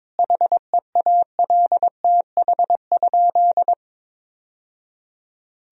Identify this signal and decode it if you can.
Morse code, heard as a keyed tone, about 22 wpm, 700 Hz. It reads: HEALTH?
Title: Sentences from Top 1000 Words - Review Part 3 - 22wpm